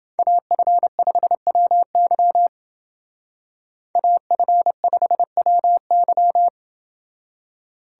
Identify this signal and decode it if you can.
Morse code, heard as a keyed tone, about 29 wpm, 700 Hz. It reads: AF5WY AF5WY